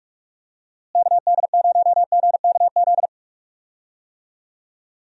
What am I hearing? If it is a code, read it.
Morse code, heard as a keyed tone, about 43 wpm, 700 Hz. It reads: KD0GKZ
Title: USA Call Signs - 6 Characters - Repeated 3x - 45wpm